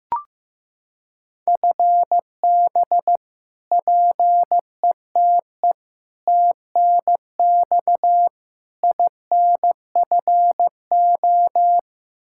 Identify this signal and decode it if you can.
Morse code, heard as a keyed tone, about 15 wpm, 700 Hz. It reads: FB PETE TNX INFO
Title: Practice QSO - Exchange 3 - More Info - 15wpm